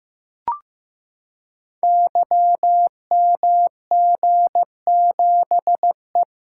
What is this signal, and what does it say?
Morse code, heard as a keyed tone, about 15 wpm, 700 Hz. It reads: YMG7E
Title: Code Groups - 5 Letter:Numbers - 15wpm